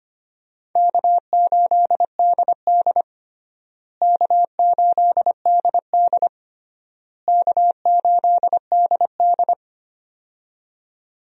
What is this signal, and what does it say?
Morse code, heard as a keyed tone, about 25 wpm, 700 Hz. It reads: K8DD K8DD K8DD
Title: USA Call Signs - 4 Characters - Repeated 3x - 25wpm